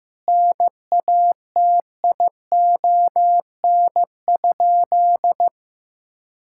Morse code, heard as a keyed tone, about 15 wpm, 700 Hz. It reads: NATION?
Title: Sentences from Top 1000 Words - Review Part 6 - 15wpm